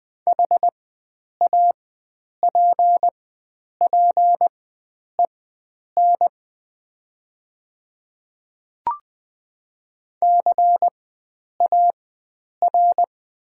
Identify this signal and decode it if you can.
Morse code, heard as a keyed tone, about 20 wpm, spaced as for 9 wpm, 700 Hz. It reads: HAPPEN CAR
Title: 4x Character Spacing - Top 500 Words - 20wpm